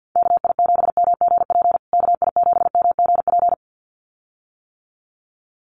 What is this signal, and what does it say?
Morse code, heard as a keyed tone, about 52 wpm, 700 Hz. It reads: KI7MGP KI7MGP
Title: USA Call Signs Repeated 2x 50wpm